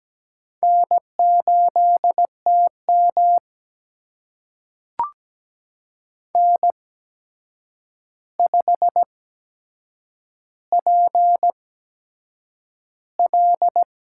Morse code, heard as a keyed tone, about 17 wpm, spaced as for 7 wpm, 700 Hz. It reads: N8TM N5PL
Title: USA Call Signs - 8x Character Spacing - 17wpm